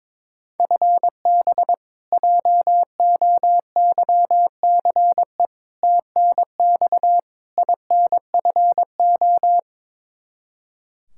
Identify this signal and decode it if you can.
Morse code, heard as a keyed tone, about 21 wpm, 700 Hz. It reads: FB JOYCE TNX INFO
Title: K1USN SST QSO Practice 22wpm